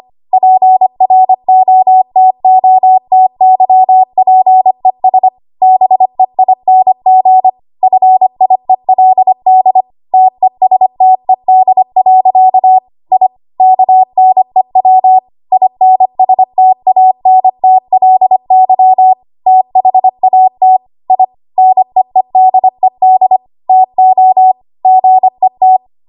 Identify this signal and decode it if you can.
Morse code, heard as a keyed tone, about 25 wpm, 750 Hz. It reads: PROTOTYPES BEING FIELD TESTED. I KNEW INSTANTLY THAT I NEEDED TO GET
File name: Archive / 25 WPM / 25WPM